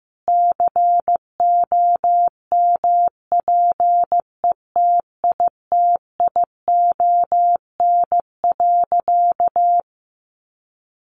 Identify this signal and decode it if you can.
Morse code, heard as a keyed tone, about 15 wpm, 700 Hz. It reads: COMPETITION.
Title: Sentences from Top 1000 Words 15wpm